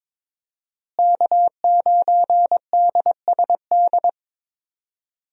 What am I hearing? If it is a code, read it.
Morse code, heard as a keyed tone, about 22 wpm, 700 Hz. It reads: K9DSD
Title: USA Call Signs - 5 Characters - Repeated 3x - 22wpm